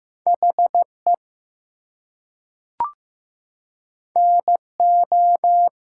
Morse code, heard as a keyed tone, about 15 wpm, 700 Hz. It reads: HE NO
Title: Two Letter Words - 15wpm